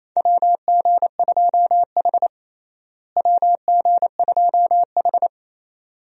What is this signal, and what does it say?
Morse code, heard as a keyed tone, about 27 wpm, 700 Hz. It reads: WG2H WG2H
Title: USA Call Signs - 4 Characters - Repeated 2x - 28wpm